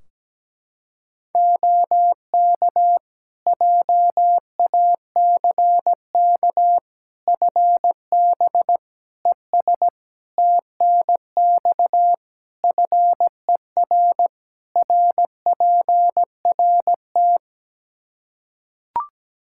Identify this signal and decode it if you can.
Morse code, heard as a keyed tone, about 17 wpm, 700 Hz. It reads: OK JACK FB ES TNX FER RPRT
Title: Practice QSO - Exchange 2 - Info - 17wpm